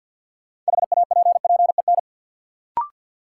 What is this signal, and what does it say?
Morse code, heard as a keyed tone, about 53 wpm, 700 Hz. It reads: HAPPEN
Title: Sets of 5 Words Encore - New Word Drill - 50wpm